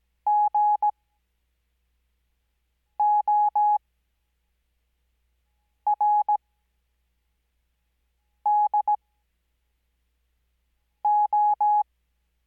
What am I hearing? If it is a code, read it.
Morse code, heard as a keyed tone, about 17 wpm, spaced as for 4 wpm, 840 Hz. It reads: GORDO
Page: Farnsworth Spacing: 15WPM Character speed Spaced at 5WPM.